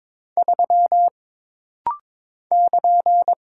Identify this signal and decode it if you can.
Morse code, heard as a keyed tone, about 22 wpm, 700 Hz. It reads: 3 <KN>
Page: This rapid-fire timing is geared to push your ICR (Instant Character Recognition) skills.